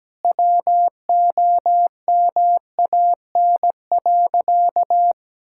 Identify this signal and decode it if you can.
Morse code, heard as a keyed tone, about 17 wpm, 700 Hz. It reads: WOMAN.
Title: Sentences from Top 1600 Words 17wpm